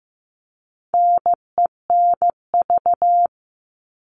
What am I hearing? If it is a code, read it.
Morse code, heard as a keyed tone, about 15 wpm, 700 Hz. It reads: NENV
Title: All Cut Numbers - 4-Digit 15wpm